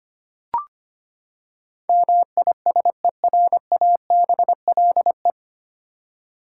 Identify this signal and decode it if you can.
Morse code, heard as a keyed tone, about 25 wpm, 700 Hz. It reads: MISERABLE
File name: Nine Letter Words - 25wpm